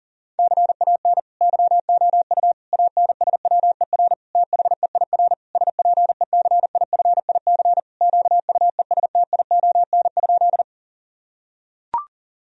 Morse code, heard as a keyed tone, about 40 wpm, 700 Hz. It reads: CAN YOU ANSWER THEIR SPECIFIC QUESTION?
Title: Sentences from Top 1000 Words - Review Part 3 - 40wpm